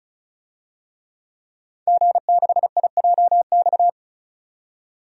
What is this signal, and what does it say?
Morse code, heard as a keyed tone, about 36 wpm, 700 Hz. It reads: G6IJX
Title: DXCC-100 Call Signs - 5 Characters - 35wpm